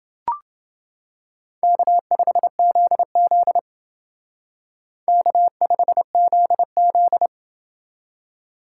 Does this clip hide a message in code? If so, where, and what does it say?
Yes: Morse code from 1.6 to 7.3 s, heard as a keyed tone, about 29 wpm, 700 Hz. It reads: K5ZZ K5ZZ